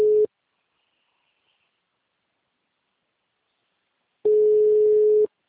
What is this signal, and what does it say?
Ringback tone (Europe, 425 Hz).
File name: out- -2003- - - .8